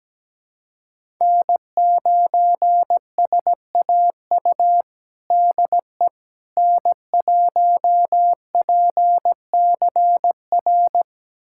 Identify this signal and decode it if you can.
Morse code, heard as a keyed tone, about 17 wpm, 700 Hz. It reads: N9SAU DE N1PCR
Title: USA Callsign DE Callsign - 5 Characters - 17wpm